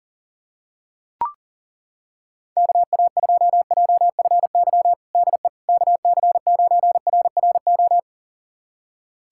Morse code, heard as a keyed tone, about 37 wpm, 700 Hz. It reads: KA2JFY DE KC9RRO
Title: USA Callsign DE Callsign - 6 Characters - 40wpm